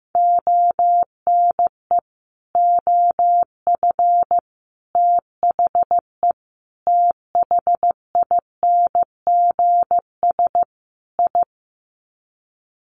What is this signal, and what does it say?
Morse code, heard as a keyed tone, about 15 wpm, 700 Hz. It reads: ONE OF THE THINGS I
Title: Sets of 5 Words 15wpm